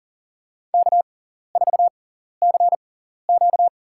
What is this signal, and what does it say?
Morse code, heard as a keyed tone, about 40 wpm, spaced as for 14 wpm, 700 Hz. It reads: K4CQ
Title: USA Call Signs - 4 Characters - 6x Character Spacing - 40wpm